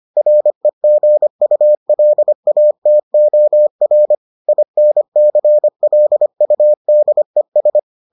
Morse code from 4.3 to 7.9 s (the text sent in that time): INCLUDES